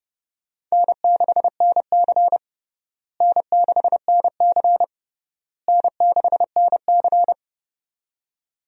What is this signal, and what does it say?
Morse code, heard as a keyed tone, about 32 wpm, 700 Hz. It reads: N6NC N6NC N6NC
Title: USA Call Signs - 4 Characters - Repeated 3x - 30wpm